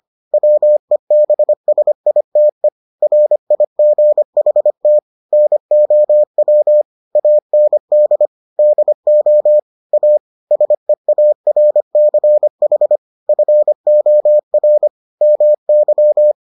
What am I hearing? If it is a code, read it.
Morse code, heard as a keyed tone, about 25 wpm, 600 Hz. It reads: WEBSITE RIGHT NOW AND DO A SEARCH FOR MY